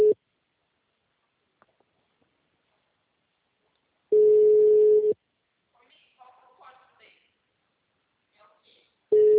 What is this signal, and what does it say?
Ringback tone (Europe, 425 Hz), with speech.